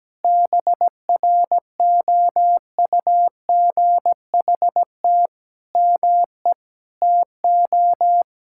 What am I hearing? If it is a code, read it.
Morse code, heard as a keyed tone, about 17 wpm, 700 Hz. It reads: BROUGHT ME TO
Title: Sentences from Top 1000 Words - Review Part 6 - 17wpm